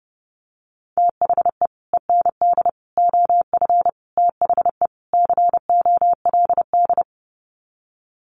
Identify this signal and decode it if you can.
Morse code, heard as a keyed tone, about 30 wpm, 700 Hz. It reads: THE END OF THE COLD
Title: Sets of 5 Words 30wpm